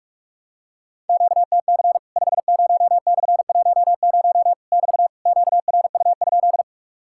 Morse code, heard as a keyed tone, about 44 wpm, 700 Hz. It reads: QTC 50/10 <BT> QRU?
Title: Worked All Europe DX Contest - QTC Exchange - 45wpm